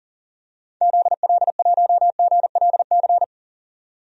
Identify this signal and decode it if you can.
Morse code, heard as a keyed tone, about 40 wpm, 700 Hz. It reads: ZL1GLC
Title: DXCC-100 Call Signs - 6 Characters - 40wpm